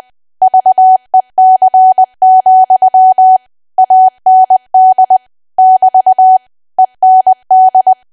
Morse code, heard as a keyed tone, about 20 wpm, 750 Hz. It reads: VEC, AND <BT> END
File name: cwa / W1AW-Level3 / 20WPM